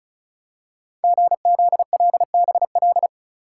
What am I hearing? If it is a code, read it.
Morse code, heard as a keyed tone, about 36 wpm, 700 Hz. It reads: GZLBL
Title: Code Groups - 5 Letters - 35wpm